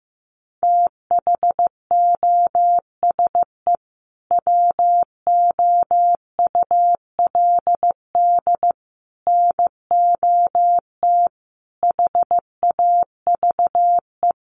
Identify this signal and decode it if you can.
Morse code, heard as a keyed tone, about 15 wpm, 700 Hz. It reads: THOSE WOULD NOT HAVE
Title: Sentences from Top 1300 Words 15wpm